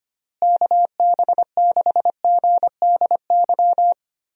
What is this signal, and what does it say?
Morse code, heard as a keyed tone, about 25 wpm, 700 Hz. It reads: KB6GDY
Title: USA Call Signs - 6 Characters - Repeated 3x - 25wpm